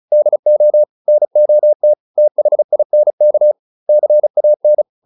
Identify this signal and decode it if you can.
Morse code, heard as a keyed tone, about 36 wpm, 600 Hz. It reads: DO NOT THINK CAN